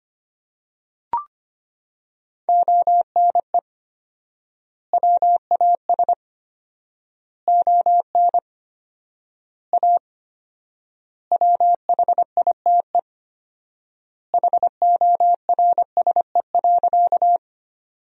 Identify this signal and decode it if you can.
Morse code, heard as a keyed tone, about 25 wpm, 700 Hz. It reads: ONE WAS ON A WHITE HORSE.
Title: 4x Word-Spacing - Sentences from Top 300 Words - 25wpm